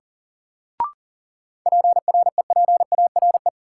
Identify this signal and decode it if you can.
Morse code, heard as a keyed tone, about 40 wpm, 700 Hz. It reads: PREPARE
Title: Sentences from Top 900 Words - New Word Drill - 40wpm